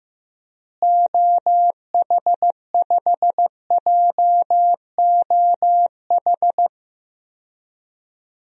Morse code, heard as a keyed tone, about 15 wpm, 700 Hz. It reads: OH5JOH
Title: DXCC-100 Call Signs - 6 Characters - Repeated 2x - 15wpm